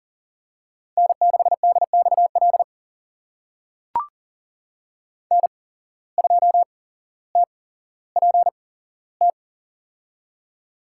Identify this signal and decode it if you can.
Morse code, heard as a keyed tone, about 37 wpm, spaced as for 17 wpm, 700 Hz. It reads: N6DXL N2TPT
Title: USA Call Signs - 5 Characters - 8x Character Spacing - 40wpm